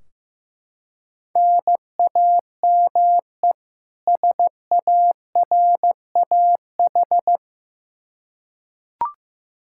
Morse code, heard as a keyed tone, about 15 wpm, 700 Hz. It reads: NAME SARAH
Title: K1USN SST QSO Practice 15wpm